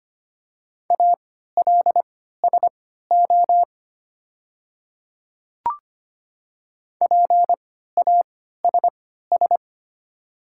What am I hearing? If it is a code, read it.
Morse code, heard as a keyed tone, about 25 wpm, spaced as for 14 wpm, 700 Hz. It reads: ALSO PASS
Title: 3x Character Spacing - Top 500 Words - 25wpm